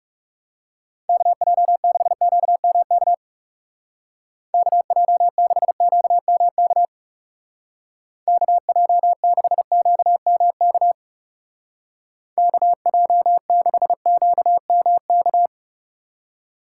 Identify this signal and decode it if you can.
Morse code, heard as a keyed tone, about 35 wpm, 700 Hz. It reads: KJ6QMK KJ6QMK KJ6QMK KJ6QMK
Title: Speed Racing - USA 2x3 Call Signs 30wpm